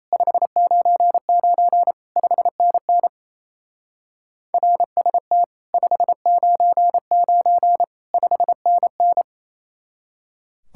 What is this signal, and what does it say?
Morse code, heard as a keyed tone, about 37 wpm, 700 Hz. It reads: 599 5NN RST 599 5NN